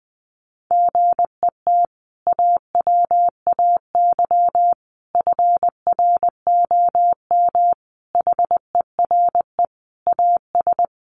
Morse code, heard as a keyed tone, about 20 wpm, 700 Hz. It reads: GET AWAY FROM HERE AS
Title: Sentences from Top 600 Words 20wpm